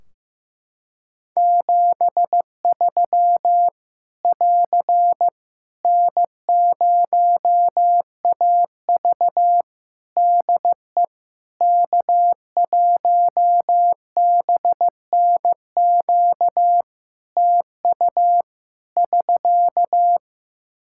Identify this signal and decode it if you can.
Morse code, heard as a keyed tone, about 15 wpm, 700 Hz. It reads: 73 <AR> N0AV DE K1BNQ TU <SK>
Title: Practice QSO - Exchange 4 - Ending - 15wpm